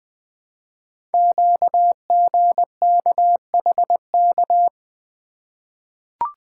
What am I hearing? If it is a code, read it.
Morse code, heard as a keyed tone, about 20 wpm, 700 Hz. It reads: QGKHK
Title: Code Groups - 5 Letters - 20wpm